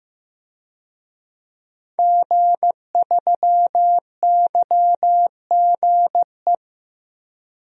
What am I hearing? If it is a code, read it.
Morse code, heard as a keyed tone, about 15 wpm, 700 Hz. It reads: G3YGE
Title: DXCC-100 Call Signs - 5 Characters - Repeated 2x - 15wpm